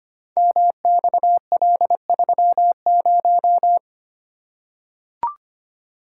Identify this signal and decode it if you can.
Morse code, heard as a keyed tone, about 25 wpm, 700 Hz. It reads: MXL30
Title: Code Groups - 5 Letter:Numbers - 25wpm